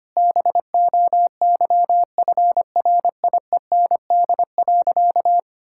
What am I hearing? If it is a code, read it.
Morse code, heard as a keyed tone, about 25 wpm, 700 Hz. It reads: BOYFRIEND.
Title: Sentences from Top 1000 Words - Review Part 5 - 25wpm